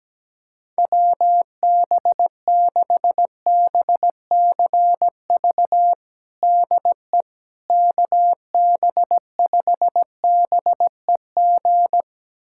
Morse code, heard as a keyed tone, about 17 wpm, 700 Hz. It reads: WB6BCV DE KB5BEG
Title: USA Callsign DE Callsign - 6 Characters - 17wpm